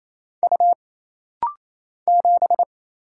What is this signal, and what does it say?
Morse code, heard as a keyed tone, about 28 wpm, 700 Hz. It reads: U 7
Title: Single Letter-Number Alt - Rapid-Fire 28wpm